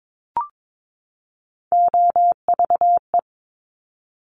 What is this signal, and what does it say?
Morse code, heard as a keyed tone, about 22 wpm, 700 Hz. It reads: OVE
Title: Top Three Letter Combinations in Words 22wpm